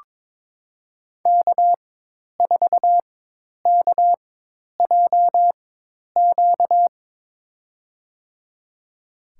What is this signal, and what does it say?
Morse code, heard as a keyed tone, about 22 wpm, spaced as for 10 wpm, 700 Hz. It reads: K4KJQ
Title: USA Call Signs - 5 Characters - 4x Character Spacing - 22wpm